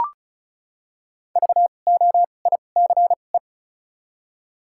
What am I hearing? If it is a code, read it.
Morse code, heard as a keyed tone, about 34 wpm, spaced as for 25 wpm, 700 Hz. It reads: VOICE
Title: 2x Character Spacing - Top 500 Words - 35wpm